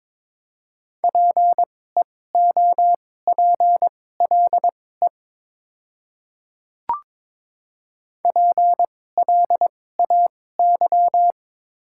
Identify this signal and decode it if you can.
Morse code, heard as a keyed tone, about 21 wpm, spaced as for 16 wpm, 700 Hz. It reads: PEOPLE PLAY
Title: 2x Character Spacing - Top 500 Words - 22wpm